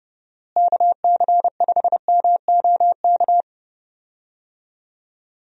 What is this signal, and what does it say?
Morse code, heard as a keyed tone, about 30 wpm, 700 Hz. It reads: KC5MOK
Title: USA Call Signs - 6 Characters - Repeated 3x - 30wpm